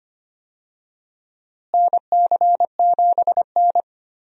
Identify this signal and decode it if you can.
Morse code, heard as a keyed tone, about 25 wpm, 700 Hz. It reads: NC7N
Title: USA Call Signs - 4 Characters - Repeated 3x - 25wpm